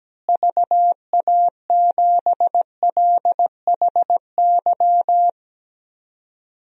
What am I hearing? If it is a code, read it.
Morse code, heard as a keyed tone, about 17 wpm, 700 Hz. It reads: VA7LHY
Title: DXCC-100 Call Signs - 6 Characters - Repeated 3x - 17wpm